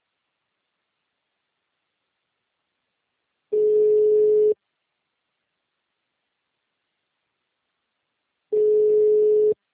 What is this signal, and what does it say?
Ringback tone (Europe, 425 Hz).
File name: out- -2005- - - .16262